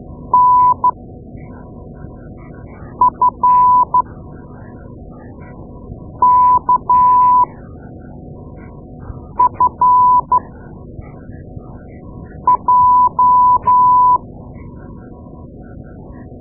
Morse code, heard as a keyed tone, about 12 wpm, spaced as for 4 wpm, 1000 Hz. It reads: NFKFJ
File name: Morse Lessons / Morse Lessons / Tape-2A